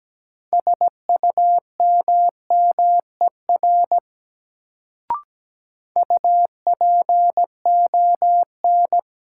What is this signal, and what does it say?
Morse code, heard as a keyed tone, about 17 wpm, 700 Hz. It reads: SUMMER UPON
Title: Sentences from Top 1000 Words - Review Part 6 - New Word Drill - 17wpm